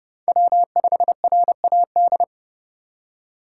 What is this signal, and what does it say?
Morse code, heard as a keyed tone, about 28 wpm, 700 Hz. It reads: W5RAD